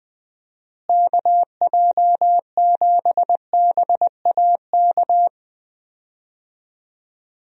Morse code, heard as a keyed tone, about 20 wpm, 700 Hz. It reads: KJ7BAK
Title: USA Call Signs - 6 Characters - Repeated 2x - 20wpm